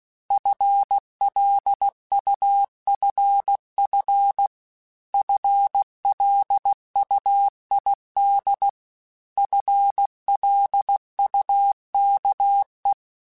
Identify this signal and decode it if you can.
Morse code, heard as a keyed tone, about 16 wpm, 800 Hz. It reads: FLUFF FLUID FLUKE
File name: FL Runs / FL Runs mp3s / FL Runs @15